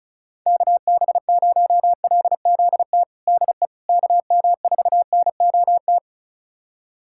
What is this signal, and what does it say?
Morse code, heard as a keyed tone, about 36 wpm, 700 Hz. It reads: KB0LZT DE KM4NOT
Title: USA Callsign DE Callsign - 6 Characters - 35wpm